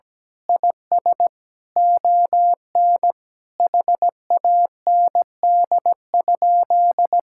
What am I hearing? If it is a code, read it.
Morse code, heard as a keyed tone, about 17 wpm, 700 Hz. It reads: IS ON HAND?
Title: Sentences from Top 1000 Words - Review Part 5 - 17wpm